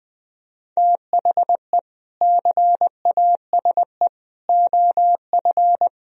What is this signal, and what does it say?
Morse code, heard as a keyed tone, about 22 wpm, 700 Hz. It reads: THE CASE OF
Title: Speed-Racing - Sets of 3 Words 20wpm